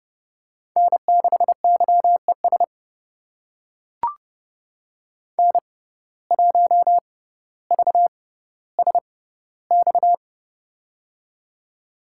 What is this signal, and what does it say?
Morse code, heard as a keyed tone, about 30 wpm, spaced as for 15 wpm, 700 Hz. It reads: N6YES N1VSX
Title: USA Call Signs - 5 Characters - 6x Character Spacing - 30wpm